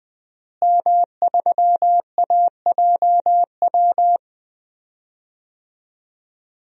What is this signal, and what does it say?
Morse code, heard as a keyed tone, about 20 wpm, 700 Hz. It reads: M3AJW
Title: DXCC-100 Call Signs - 5 Characters - Repeated 2x - 20wpm